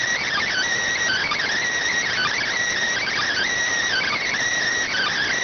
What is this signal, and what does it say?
Emergency Alert System (SAME) tones, heard as harsh buzzing bursts. It is heard from 0.2 to 4.3 s.